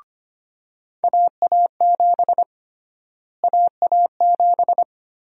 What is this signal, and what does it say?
Morse code, heard as a keyed tone, about 25 wpm, 700 Hz. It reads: AA7 AA7
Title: USA Call Sign Prefixes - Repeated 2x - 25wpm